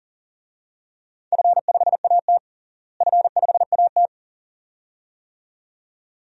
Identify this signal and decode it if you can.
Morse code, heard as a keyed tone, about 40 wpm, 700 Hz. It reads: F5AT F5AT
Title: DXCC-100 Call Signs - 4 Characters - Repeated 2x - 40wpm